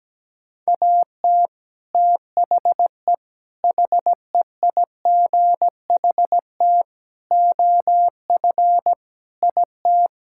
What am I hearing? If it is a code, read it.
Morse code, heard as a keyed tone, about 17 wpm, 700 Hz. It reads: AT THE HEIGHT OF IT
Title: Sentences from Top 1600 Words 17wpm